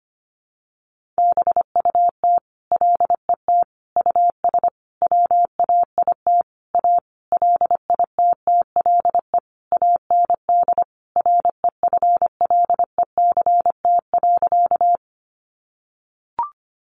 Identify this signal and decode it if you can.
Morse code, heard as a keyed tone, about 25 wpm, 700 Hz. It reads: BUT LET US WAIT A LITTLE AND REFLECT.
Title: Sentences from Top 1500 Words 25wpm